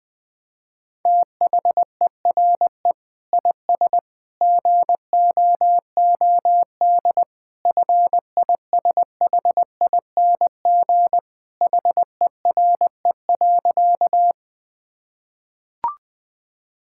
Morse code, heard as a keyed tone, about 19 wpm, 700 Hz. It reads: THERE IS GOOD FISHING HERE.
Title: Sentences from Top 1000 Words - Review Part 5 - 20wpm